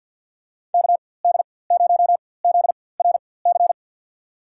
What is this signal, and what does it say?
Morse code, heard as a keyed tone, about 51 wpm, spaced as for 23 wpm, 700 Hz. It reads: KD0ZRC
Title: USA Call Signs - 6 Characters - 4x Character Spacing - 50wpm